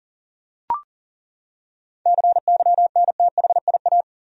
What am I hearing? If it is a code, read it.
Morse code, heard as a keyed tone, about 40 wpm, 700 Hz. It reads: CYNTHIA
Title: USA Common Names - 40wpm